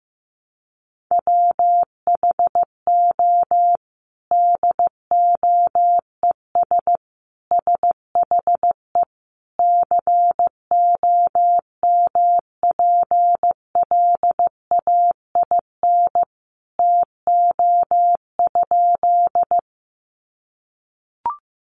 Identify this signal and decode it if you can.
Morse code, heard as a keyed tone, about 14 wpm, 700 Hz. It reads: WHO DOES SHE COMPLAIN TO?
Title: Sentences from Top 1300 Words 15wpm